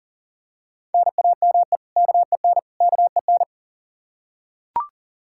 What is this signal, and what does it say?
Morse code, heard as a keyed tone, about 37 wpm, 700 Hz. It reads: NAME KEN KEN
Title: Practice QSO - Exchange 1 - Beginning - 40wpm